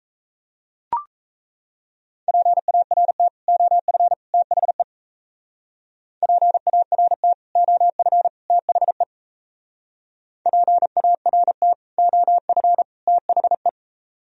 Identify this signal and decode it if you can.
Morse code, heard as a keyed tone, about 35 wpm, 700 Hz. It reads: PART OF THE PART OF THE PART OF THE
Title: Speed-Racing - Sets of 3 Words 28wpm